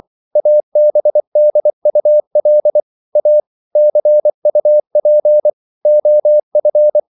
Morse code, heard as a keyed tone, about 24 wpm, 600 Hz. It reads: ABDUL A CUP OF